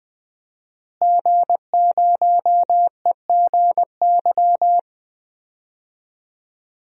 Morse code, heard as a keyed tone, about 20 wpm, 700 Hz. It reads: G0EGY
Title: DXCC-100 Call Signs - 5 Characters - Repeated 3x - 20wpm